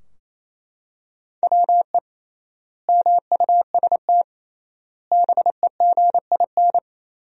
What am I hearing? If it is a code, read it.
Morse code, heard as a keyed tone, about 27 wpm, 700 Hz. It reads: WE MUST BEGIN
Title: 3x Word-Spacing - Sentences from Top 300 Words - 28wpm